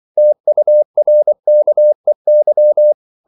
Morse code, heard as a keyed tone, about 24 wpm, 600 Hz. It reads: TURKEY
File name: CW / Practice / NationalCapitals24